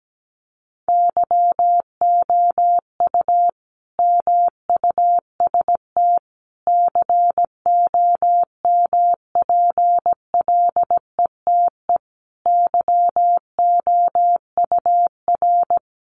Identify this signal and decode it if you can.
Morse code, heard as a keyed tone, about 17 wpm, 700 Hz. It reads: YOU MUST COMPLETE YOUR
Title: Sentences from Top 1400 Words 17wpm